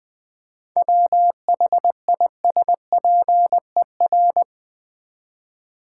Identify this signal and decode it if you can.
Morse code, heard as a keyed tone, about 20 wpm, 700 Hz. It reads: WHISPER
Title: 7 Letter Words - First 3 Letters Given - 20wpm